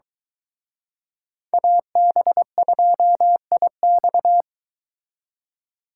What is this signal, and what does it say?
Morse code, heard as a keyed tone, about 23 wpm, 700 Hz. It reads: AB2IX